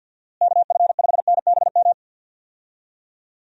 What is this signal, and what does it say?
Morse code, heard as a keyed tone, about 48 wpm, 700 Hz. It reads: KF5NBM
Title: USA Call Signs - 6 Characters - Repeated 2x - 50wpm